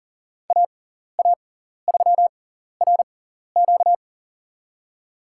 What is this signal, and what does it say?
Morse code, heard as a keyed tone, about 40 wpm, spaced as for 14 wpm, 700 Hz. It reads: AA3RQ
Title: USA Call Signs - 6x Character Spacing - 40wpm